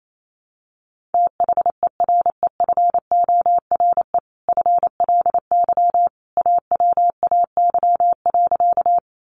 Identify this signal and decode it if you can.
Morse code, heard as a keyed tone, about 28 wpm, 700 Hz. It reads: THEREFORE FLY AWAY.
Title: Sentences from Top 800 Words 28wpm